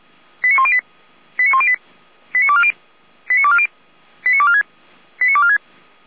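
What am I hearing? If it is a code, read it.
Five-tone selective call (ZVEI), heard as quick runs of five tones, tones of about 70 ms. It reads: 89108 89108 89240 89240 89246 89246
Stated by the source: ZVEI five-tone selective call received by SDR, archived by the Orlamünde fire brigade (Feuerwehr Orlamünde).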